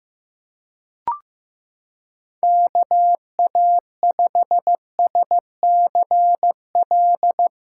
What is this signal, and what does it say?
Morse code, heard as a keyed tone, about 16 wpm, 700 Hz. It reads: KA5SCL
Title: USA Callsign DE Callsign - 6 Characters - 15wpm